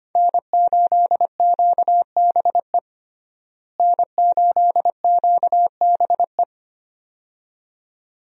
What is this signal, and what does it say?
Morse code, heard as a keyed tone, about 25 wpm, 700 Hz. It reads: N8QBE N8QBE
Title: USA Call Signs - 5 Characters - Repeated 2x - 25wpm